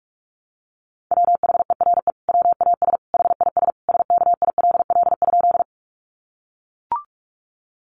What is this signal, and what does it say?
Morse code, heard as a keyed tone, about 44 wpm, 700 Hz. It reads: WHERE WAS HIS SKILL?
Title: Sentences from Top 1200 Words 45wpm